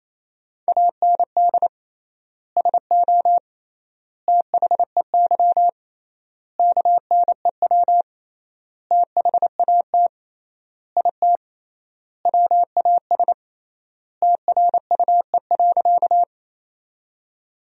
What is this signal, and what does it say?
Morse code, heard as a keyed tone, about 28 wpm, 700 Hz. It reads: AND SO THEY KNEW THAT IT WAS TRUE.
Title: 3x Word-Spacing - Sentences from Top 400 Words - 28wpm